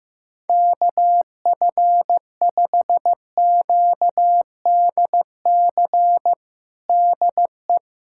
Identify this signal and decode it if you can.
Morse code, heard as a keyed tone, about 15 wpm, 700 Hz. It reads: KF5QDC DE
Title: USA Callsign DE Callsign 15wpm